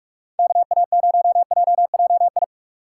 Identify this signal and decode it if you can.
Morse code, heard as a keyed tone, about 54 wpm, 700 Hz. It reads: KA0JJI
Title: Speed Racing - USA Call Signs 45wpm